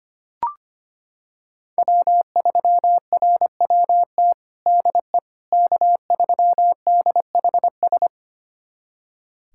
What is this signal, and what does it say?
Morse code, heard as a keyed tone, about 25 wpm, 700 Hz. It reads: W3RWT DE K3DHS
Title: USA Callsign DE Callsign - 5 Characters - 25wpm